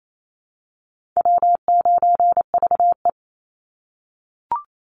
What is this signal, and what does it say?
Morse code, heard as a keyed tone, about 28 wpm, 700 Hz. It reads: W9VE
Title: Easy Call Signs - 28wpm